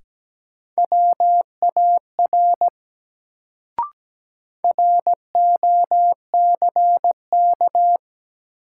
Morse code, heard as a keyed tone, about 17 wpm, 700 Hz. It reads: WAR ROCK
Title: Sentences from Top 400 Words - New Word Drill - 17wpm